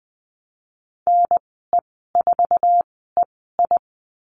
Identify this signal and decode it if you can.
Morse code, heard as a keyed tone, about 20 wpm, spaced as for 15 wpm, 700 Hz. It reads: NE4EI
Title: 161 Lesson 26g - Letter g - Call Signs 20wpm